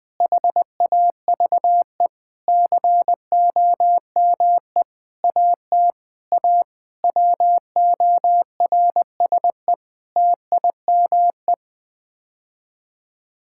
Morse code, heard as a keyed tone, about 20 wpm, 700 Hz. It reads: HAVE COME AT A WORSE TIME
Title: Sets of 8 Words - 20wpm